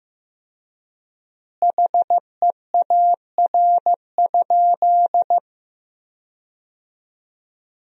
Morse code, heard as a keyed tone, about 15 wpm, 700 Hz. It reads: HEAR?
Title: 3x Word-Spacing - Sentences from Top 300 Words - 15wpm